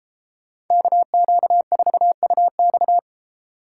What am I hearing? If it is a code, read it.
Morse code, heard as a keyed tone, about 32 wpm, 700 Hz. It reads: KQ4UX